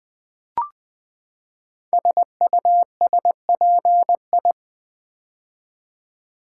Morse code, heard as a keyed tone, about 20 wpm, 700 Hz. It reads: SUSPI
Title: Build A Word - 5 Letters At A Time - 10-Letter Words - 20wpm